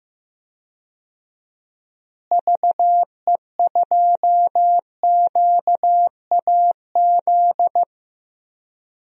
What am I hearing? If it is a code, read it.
Morse code, heard as a keyed tone, about 15 wpm, 700 Hz. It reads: VE2QAZ
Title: DXCC-100 Call Signs - 6 Characters - Repeated 2x - 15wpm